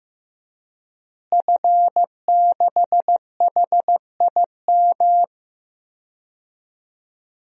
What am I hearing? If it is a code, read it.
Morse code, heard as a keyed tone, about 15 wpm, 700 Hz. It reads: F6HIM
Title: DXCC-100 Call Signs - 5 Characters - Repeated 2x - 15wpm